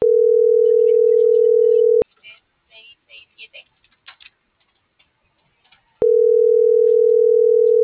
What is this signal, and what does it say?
Ringback tone (North America), 440 + 480 Hz.